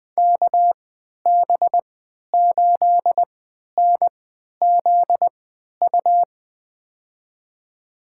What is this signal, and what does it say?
Morse code, heard as a keyed tone, about 21 wpm, spaced as for 11 wpm, 700 Hz. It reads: KB8NZU
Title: USA Call Signs - 6 Characters - 3x Character Spacing - 20wpm